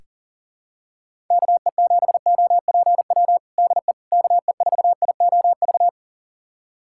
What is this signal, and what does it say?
Morse code, heard as a keyed tone, about 37 wpm, 700 Hz. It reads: KE7OPW DE KE4IOV
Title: USA Callsign DE Callsign - 6 Characters - 40wpm